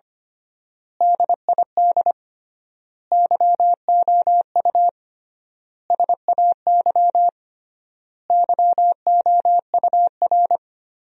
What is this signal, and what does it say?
Morse code, heard as a keyed tone, about 25 wpm, 700 Hz. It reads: DID YOU SAY YOUR
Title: 3x Word-Spacing - Sentences from Top 200 Words - 25wpm